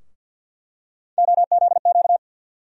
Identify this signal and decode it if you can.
Morse code, heard as a keyed tone, about 46 wpm, 700 Hz. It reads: OZQ
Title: USA Call Sign Postfixes - 3 Letters 50wpm